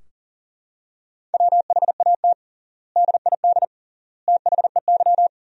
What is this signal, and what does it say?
Morse code, heard as a keyed tone, about 37 wpm, 700 Hz. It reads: WHAT DID THEY
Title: 3x Word-Spacing - Sentences from Top 100 Words - 40wpm